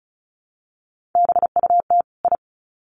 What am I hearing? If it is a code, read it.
Morse code, heard as a keyed tone, about 34 wpm, 700 Hz. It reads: BUT I
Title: Sets of 2 Words 35wpm